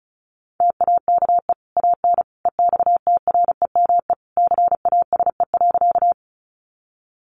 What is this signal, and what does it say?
Morse code, heard as a keyed tone, about 34 wpm, 700 Hz. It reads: TAKE AN EXTREME CASE.
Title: Sentences from Top 1300 Words 35wpm